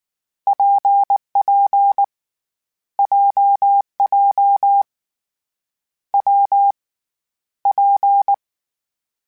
Morse code, heard as a keyed tone, about 19 wpm, 800 Hz. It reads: PP JJ W P
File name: Variable Speed / Variable Speed-mp3s / WPJ WPJ1 sp @18